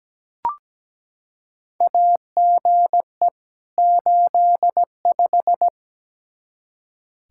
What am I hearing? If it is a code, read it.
Morse code, heard as a keyed tone, about 17 wpm, 700 Hz. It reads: AGE 85
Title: Practice QSO - Exchange 3 - More Info - 17wpm